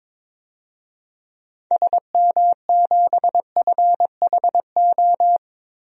Morse code, heard as a keyed tone, about 22 wpm, 700 Hz. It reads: SM7FHO
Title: DXCC-100 Call Signs - 6 Characters - Repeated 3x - 22wpm